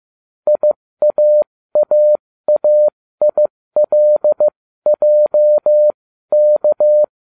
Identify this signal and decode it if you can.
Morse code, heard as a keyed tone, about 15 wpm, spaced as for 13 wpm, 600 Hz. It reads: IAAAILJK